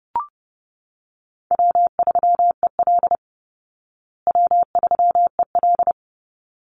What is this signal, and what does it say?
Morse code, heard as a keyed tone, about 30 wpm, 700 Hz. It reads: W3EL W3EL
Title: Easy Call Signs Repeated 2x - 30wpm